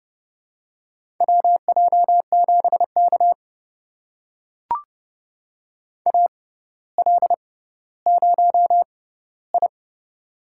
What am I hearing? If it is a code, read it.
Morse code, heard as a keyed tone, about 29 wpm, spaced as for 15 wpm, 700 Hz. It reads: WJ7K AL0I
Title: USA Call Signs - 4 Characters - 6x Character Spacing - 30wpm